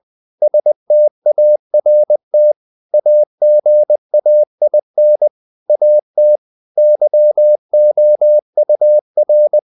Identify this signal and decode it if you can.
Morse code, heard as a keyed tone, about 20 wpm, 600 Hz. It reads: START AGAIN AT YOUR
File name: CW / Practice / If20